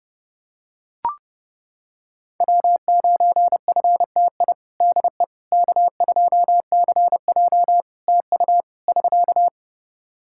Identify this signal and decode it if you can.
Morse code, heard as a keyed tone, about 30 wpm, 700 Hz. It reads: W9FTI DE K2CJ TU <SK>
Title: K1USN SST QSO Practice 30wpm